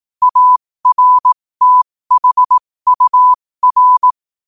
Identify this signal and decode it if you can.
Morse code, heard as a keyed tone, about 18 wpm, spaced as for 15 wpm, 1000 Hz. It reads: ARTHUR
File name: qotdfiles / 15wpm